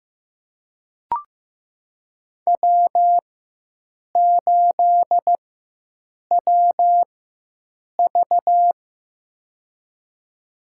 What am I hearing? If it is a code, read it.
Morse code, heard as a keyed tone, about 15 wpm, spaced as for 7 wpm, 700 Hz. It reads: W8WV
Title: USA Call Signs - 4 Characters - 4x Character Spacing - 15wpm